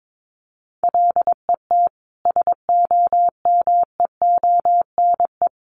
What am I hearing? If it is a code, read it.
Morse code, heard as a keyed tone, about 22 wpm, 700 Hz. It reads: LET SOMEONE
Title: Sentences from Top 1200 Words 22wpm